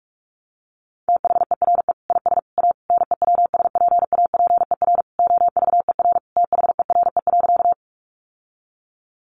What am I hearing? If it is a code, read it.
Morse code, heard as a keyed tone, about 44 wpm, 700 Hz. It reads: THERE IS A NEWSPAPER OVER THERE.
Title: Sentences from Top 1100 Words 45wpm